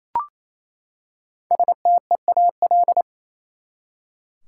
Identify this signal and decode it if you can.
Morse code, heard as a keyed tone, about 28 wpm, 700 Hz. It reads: STEAL
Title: Five Letter Words - 28wpm